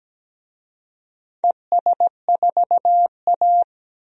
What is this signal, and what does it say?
Morse code, heard as a keyed tone, about 17 wpm, 700 Hz. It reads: ES4A
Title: DXCC-100 Call Signs - 4 Characters - 17wpm